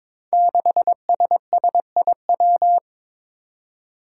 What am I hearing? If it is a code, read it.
Morse code, heard as a keyed tone, about 22 wpm, 700 Hz. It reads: 6SSIW
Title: Code Groups - 5 Letter:Numbers - 22wpm